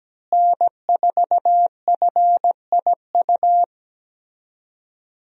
Morse code, heard as a keyed tone, about 17 wpm, 700 Hz. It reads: N4FIU
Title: USA Call Signs - 5 Characters - Repeated 2x - 17wpm